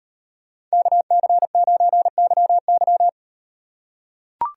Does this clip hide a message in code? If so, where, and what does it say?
Yes: Morse code from 0.7 to 3.1 s, heard as a keyed tone, about 36 wpm, 700 Hz. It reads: KC9YY